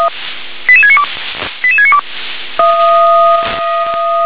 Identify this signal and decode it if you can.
Five-tone selective call (ZVEI), heard as quick runs of five tones, tones of about 70 ms. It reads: 88692 88692
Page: SDR/ZVEI